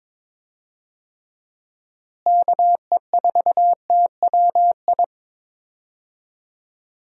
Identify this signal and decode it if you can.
Morse code, heard as a keyed tone, about 22 wpm, 700 Hz. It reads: KE4TWI
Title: USA Call Signs - 6 Characters - Repeated 3x - 22wpm